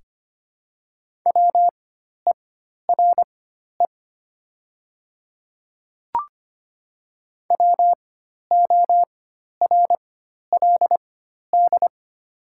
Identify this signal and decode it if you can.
Morse code, heard as a keyed tone, about 25 wpm, spaced as for 12 wpm, 700 Hz. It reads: WERE WORLD
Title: 4x Character Spacing - Top 200 Words - 25wpm